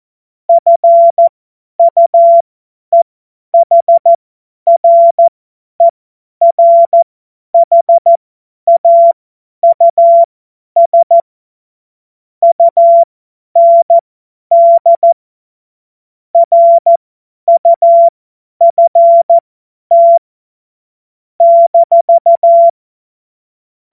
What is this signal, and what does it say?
Morse code, heard as a keyed tone, about 14 wpm, spaced as for 10 wpm, 680 Hz. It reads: FUEHRERHAUS UND RUFT -
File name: Morsekurs / Witze / Witz 01. März 2023 50 F70